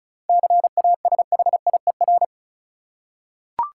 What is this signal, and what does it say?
Morse code, heard as a keyed tone, about 36 wpm, 700 Hz. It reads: CASHIER
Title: Practice QSO - Exchange 3 - More Info - 35wpm